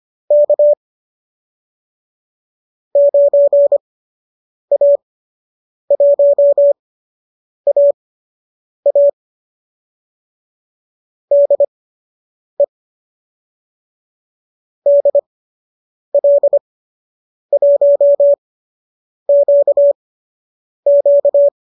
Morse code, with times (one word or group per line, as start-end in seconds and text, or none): 0.3-0.7 K
3.0-9.1 9A1AA
11.3-12.6 DE
14.9-21.5 DL1QQ